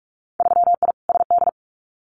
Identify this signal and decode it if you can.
Morse code, heard as a keyed tone, about 46 wpm, 700 Hz. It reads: 3I SD
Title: Winter Field Day Exchanges Encore 45wpm